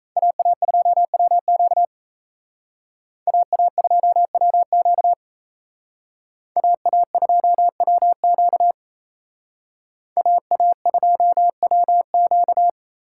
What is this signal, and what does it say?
Morse code, heard as a keyed tone, about 34 wpm, 700 Hz. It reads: AA2WQ AA2WQ AA2WQ AA2WQ
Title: Speed Racing - USA 2x2 Call Signs 28wpm